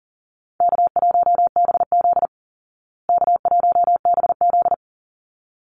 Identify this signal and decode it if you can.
Morse code, heard as a keyed tone, about 40 wpm, 700 Hz. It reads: K1BZ K1BZ
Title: Easy Call Signs Repeated 2x - 40wpm